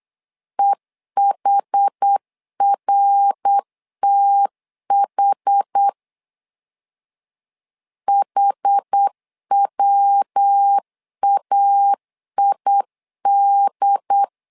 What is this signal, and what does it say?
Morse code, heard as a keyed tone, about 8 wpm, 800 Hz. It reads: EHRTH HWAID
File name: 4 Q 8wpm